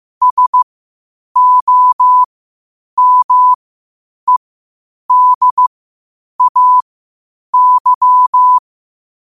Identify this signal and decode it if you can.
Morse code, heard as a keyed tone, about 15 wpm, spaced as for 8 wpm, 1000 Hz. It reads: SOMEDAY
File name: qotdfiles / 8wpm